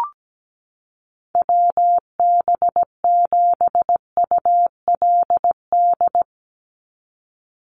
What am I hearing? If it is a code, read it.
Morse code, heard as a keyed tone, about 17 wpm, 700 Hz. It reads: WB7ULD
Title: NA SOTA Chaser Callsigns 17wpm